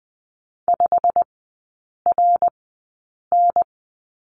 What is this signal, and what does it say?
Morse code, heard as a keyed tone, about 20 wpm, spaced as for 8 wpm, 690 Hz. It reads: 5RN
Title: 087 Lesson 15e - Number 5 - Three Characters 20wpm